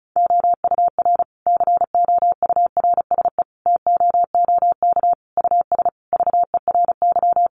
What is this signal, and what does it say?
Morse code, heard as a keyed tone, about 34 wpm, 700 Hz. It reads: OUR COURSE TOOK US VERY
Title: Sentences from Top 1300 Words 35wpm